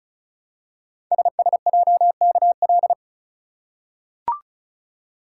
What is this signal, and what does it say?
Morse code, heard as a keyed tone, about 36 wpm, 700 Hz. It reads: SSJKL
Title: Code Groups - 5 Letters - 35wpm